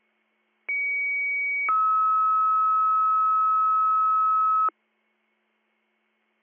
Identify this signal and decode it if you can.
Two-tone page, heard as a one-second tone, then a three-second tone, about 2250 then 1290 Hz. It is heard from 0.7 to 4.7 s.